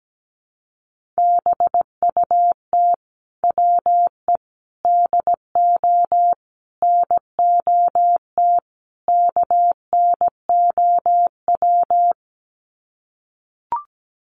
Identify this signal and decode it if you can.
Morse code, heard as a keyed tone, about 17 wpm, 700 Hz. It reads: BUT WE DO NOT KNOW
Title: Sets of 5 Words Encore2 17wpm